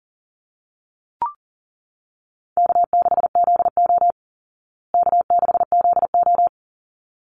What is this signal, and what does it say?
Morse code, heard as a keyed tone, about 40 wpm, 700 Hz. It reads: K6ZO K6ZO
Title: Easy Call Signs Repeated 2x - 40wpm